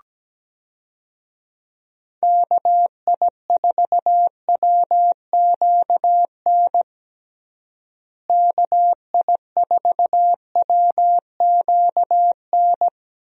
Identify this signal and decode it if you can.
Morse code, heard as a keyed tone, about 17 wpm, 700 Hz. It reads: KI4WQN KI4WQN
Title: USA Call Signs - 6 Characters - Repeated 2x - 17wpm